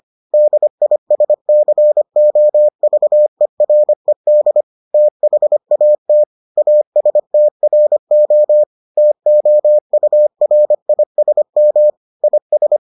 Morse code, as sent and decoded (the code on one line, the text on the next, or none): -.. .. ... -.-. --- ...- . .-. -.. / - .... .- - / .- ... - .-. --- / - --- ..- .-. .. ... -- / .. ...
DISCOVERED THAT ASTRO TOURISM IS